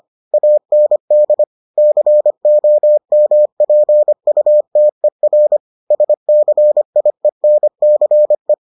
Morse code, heard as a keyed tone, about 25 wpm, 600 Hz. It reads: AND COMPUTER SCIENCE